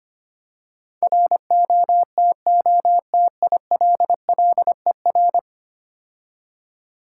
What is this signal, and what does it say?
Morse code, heard as a keyed tone, about 26 wpm, 700 Hz. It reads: ROTOTILLER
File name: practice / 25wpm / Difficult Words - Speed-Racing with Hints - 25wpm